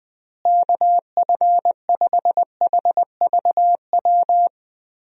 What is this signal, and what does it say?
Morse code, heard as a keyed tone, about 20 wpm, 700 Hz. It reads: KF5HVW
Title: USA Call Signs - 6 Characters - Repeated 3x - 20wpm